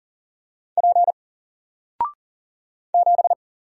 Morse code, heard as a keyed tone, about 40 wpm, 700 Hz. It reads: P 7
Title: Single Character (Letter:Number:Punctuation:Prosign) Alt 40wpm